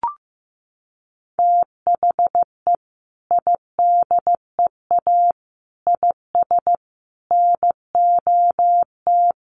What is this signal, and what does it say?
Morse code, heard as a keyed tone, about 14 wpm, 700 Hz. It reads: THE IDEA IS NOT
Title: Sentences from Top 1300 Words 15wpm